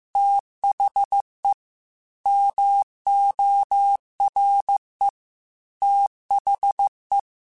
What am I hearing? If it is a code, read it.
Morse code, heard as a keyed tone, about 15 wpm, 800 Hz. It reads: THE MORE THE